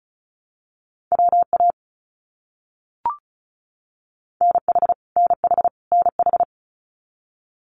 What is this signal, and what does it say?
Morse code, heard as a keyed tone, about 34 wpm, 700 Hz. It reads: WA NH NH NH
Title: US States Repeated 3x 35wpm